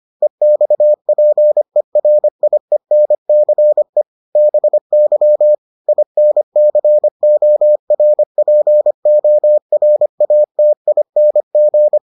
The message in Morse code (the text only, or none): EXPERIENCE BY INCORPORATING